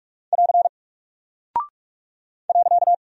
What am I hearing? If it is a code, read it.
Morse code, heard as a keyed tone, about 44 wpm, 700 Hz. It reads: <AR> .
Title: Single Character (Letter:Number:Punctuation:Prosign) 45wpm